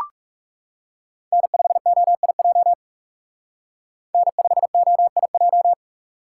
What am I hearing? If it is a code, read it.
Morse code, heard as a keyed tone, about 36 wpm, 700 Hz. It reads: N5OIJ N5OIJ
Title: Speed Racing - USA 1x3 Call Signs 30wpm